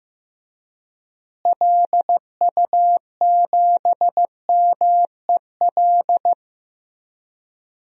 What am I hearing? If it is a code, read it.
Morse code, heard as a keyed tone, about 15 wpm, 700 Hz. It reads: LU7MEL
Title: DXCC-100 Call Signs - 6 Characters - Repeated 3x - 15wpm